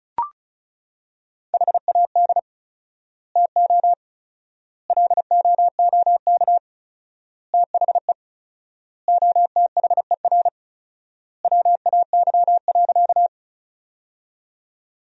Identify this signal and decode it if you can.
Morse code, heard as a keyed tone, about 36 wpm, 700 Hz. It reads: HAD TO LOOK THE OTHER WAY.
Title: 4x Word-Spacing - Sentences from Top 100 Words - 35wpm